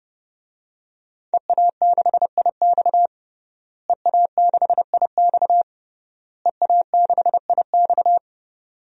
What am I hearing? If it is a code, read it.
Morse code, heard as a keyed tone, about 30 wpm, 700 Hz. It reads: EA6IX EA6IX EA6IX
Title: DXCC-100 Call Signs - 5 Characters - Repeated 3x - 30wpm